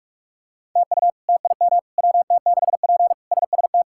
Morse code, heard as a keyed tone, about 46 wpm, 700 Hz. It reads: TU TIM WT6P SST
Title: K1USN SST QSO Practice - 45wpm